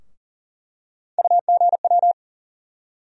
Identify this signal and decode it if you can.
Morse code, heard as a keyed tone, about 40 wpm, 700 Hz. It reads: UGW
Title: USA Call Sign Postfixes - 3 Letters 40wpm